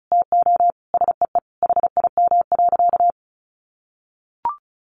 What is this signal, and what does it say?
Morse code, heard as a keyed tone, about 34 wpm, 700 Hz. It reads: TO SEE HIM.
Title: Sentences from Top 900 Words 35wpm